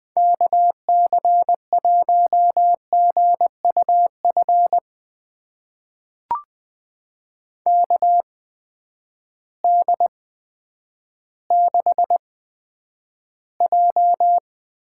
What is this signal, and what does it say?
Morse code, heard as a keyed tone, about 20 wpm, spaced as for 10 wpm, 700 Hz. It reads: KC1GUF KD6J
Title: USA Call Signs - 8x Character Spacing - 20wpm